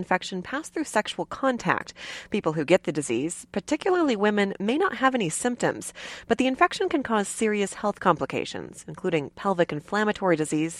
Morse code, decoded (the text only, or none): none